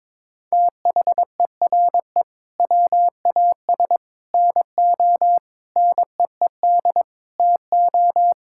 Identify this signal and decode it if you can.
Morse code, heard as a keyed tone, about 22 wpm, 700 Hz. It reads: THERE WAS NO NEED TO
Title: Sentences from Top 1600 Words 22wpm